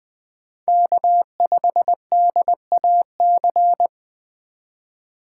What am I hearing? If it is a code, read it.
Morse code, heard as a keyed tone, about 20 wpm, 700 Hz. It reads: K5DAC
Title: USA Call Signs - 5 Characters - 3x Character Spacing - 20wpm